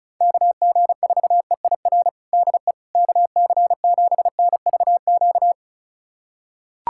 Morse code, heard as a keyed tone, about 36 wpm, 700 Hz. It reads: KG4EIR DE KC7NVQ
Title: USA Callsign DE Callsign 35wpm